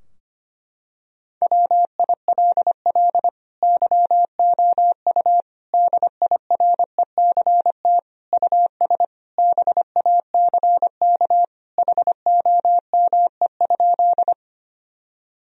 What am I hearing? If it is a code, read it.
Morse code, heard as a keyed tone, about 25 wpm, 700 Hz. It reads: WILL YOU DIRECT US BACK HOME?
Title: Sentences from Top 1000 Words - Review Part 2 - 25wpm